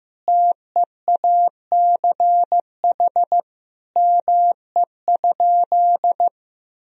Morse code, heard as a keyed tone, about 15 wpm, 700 Hz. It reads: TEACH ME?
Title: Sentences from Top 1000 Words - Review Part 2 - 15wpm